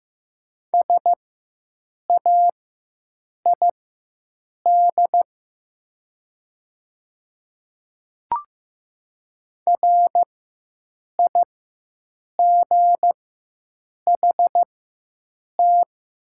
Morse code, heard as a keyed tone, about 15 wpm, spaced as for 7 wpm, 700 Hz. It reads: SAID RIGHT
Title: 4x Character Spacing - Top 200 Words - 15wpm